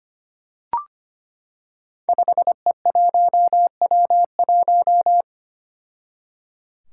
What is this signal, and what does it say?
Morse code, heard as a keyed tone, about 25 wpm, 700 Hz. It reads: 5E1W1
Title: Five Challenging Letter-Number Combinations 25wpm